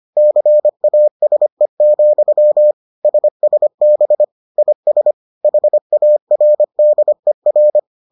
Morse code, heard as a keyed tone, about 25 wpm, 600 Hz. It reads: CASE, SSB IS HARDER